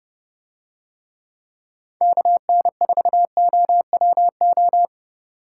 Morse code, heard as a keyed tone, about 29 wpm, 700 Hz. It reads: KN4OWO